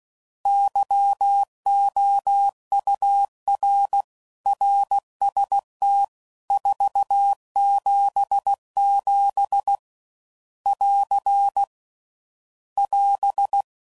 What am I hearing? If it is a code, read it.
Morse code, heard as a keyed tone, about 16 wpm, 800 Hz. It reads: YOUR RST 477 <AR> <AS>
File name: CW Code Groups @15WPM / RSTs / Your RSTs